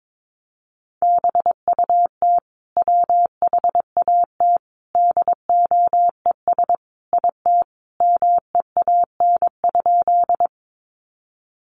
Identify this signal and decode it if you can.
Morse code, heard as a keyed tone, about 22 wpm, 700 Hz. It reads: BUT WHAT DOES IT MEAN?
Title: Sentences from Top 200 Words 22wpm